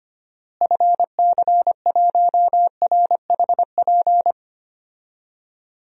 Morse code, heard as a keyed tone, about 25 wpm, 700 Hz. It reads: FC1RHP
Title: DXCC-100 Call Signs - 6 Characters - Repeated 3x - 25wpm